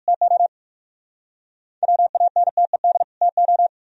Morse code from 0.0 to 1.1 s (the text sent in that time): TO